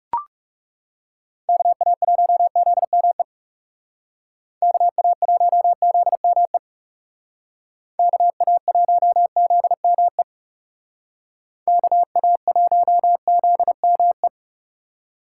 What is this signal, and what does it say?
Morse code, heard as a keyed tone, about 36 wpm, 700 Hz. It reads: KA1ZME KA1ZME KA1ZME KA1ZME
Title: Speed Racing - USA 2x3 Call Signs 30wpm